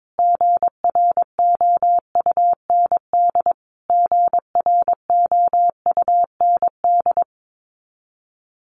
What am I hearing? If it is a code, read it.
Morse code, heard as a keyed tone, about 22 wpm, 700 Hz. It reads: GROUND GROUND
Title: Top 500 Words Repeated 22wpm